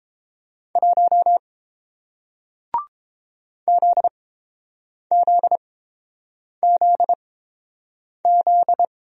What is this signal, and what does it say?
Morse code, heard as a keyed tone, about 30 wpm, spaced as for 7 wpm, 700 Hz. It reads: 1 ZZZZ